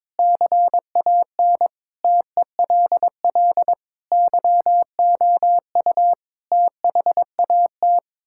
Morse code, heard as a keyed tone, about 22 wpm, 700 Hz. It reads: CAN TELL YOU THAT
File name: practice / 22wpm / Build A Phrase - 5-Word Phrases - 22wpm